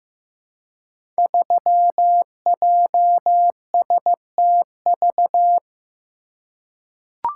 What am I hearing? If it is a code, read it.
Morse code, heard as a keyed tone, about 15 wpm, 700 Hz. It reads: 3JSTV
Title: Code Groups - 5 Letter:Numbers - 15wpm